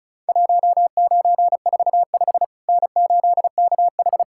Morse code, heard as a keyed tone, about 36 wpm, 700 Hz. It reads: 1945 N8KH
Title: Worked All Europe DX Contest - QTC Exchange - 35wpm